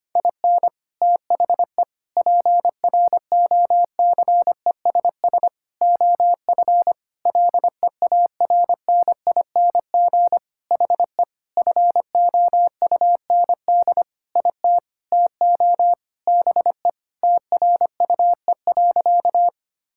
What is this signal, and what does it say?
Morse code, heard as a keyed tone, about 25 wpm, 700 Hz. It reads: IN THE PROCESS OF LEARNING HE FOUND IT TO BE TRUE.
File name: practice / 25wpm / Sentences with 4-Word Phrases - 25wpm